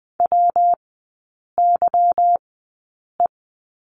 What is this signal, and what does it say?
Morse code, heard as a keyed tone, about 20 wpm, spaced as for 8 wpm, 700 Hz. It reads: WYE
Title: 152 Lesson 25e - Letter p - Three Characters 20wpm